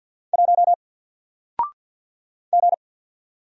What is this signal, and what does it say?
Morse code, heard as a keyed tone, about 48 wpm, 700 Hz. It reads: 1 G
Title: Single Letter-Number 50wpm